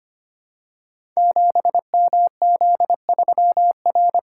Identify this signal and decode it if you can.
Morse code, heard as a keyed tone, about 25 wpm, 700 Hz. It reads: 7MZ3R
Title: Code Groups - 5 Letter:Numbers - 25wpm